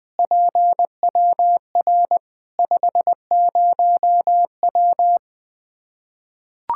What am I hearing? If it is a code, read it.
Morse code, heard as a keyed tone, about 20 wpm, 700 Hz. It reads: PWR 50W
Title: Practice QSO - Exchange 2 - Info - 20wpm